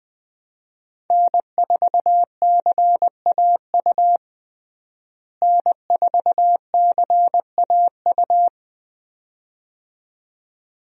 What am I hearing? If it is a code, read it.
Morse code, heard as a keyed tone, about 20 wpm, 700 Hz. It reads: N4CAU N4CAU
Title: USA Call Signs - 5 Characters - Repeated 2x - 20wpm